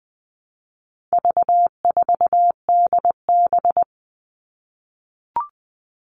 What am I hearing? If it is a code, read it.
Morse code, heard as a keyed tone, about 20 wpm, 700 Hz. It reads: V4DB
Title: All Cut Numbers - 4-Digit 20wpm